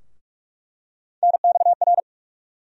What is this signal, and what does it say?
Morse code, heard as a keyed tone, about 46 wpm, 700 Hz. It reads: NXR
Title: USA Call Sign Postfixes - 3 Letters 45wpm